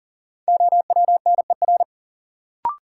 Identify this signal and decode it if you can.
Morse code, heard as a keyed tone, about 40 wpm, 700 Hz. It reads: OWNER
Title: Sentences from Top 1600 Words - New Word Drill - 40wpm